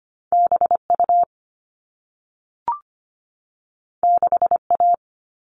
Morse code, heard as a keyed tone, about 25 wpm, 700 Hz. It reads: B U 6 A
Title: All Cut Numbers - 2-Digit 25wpm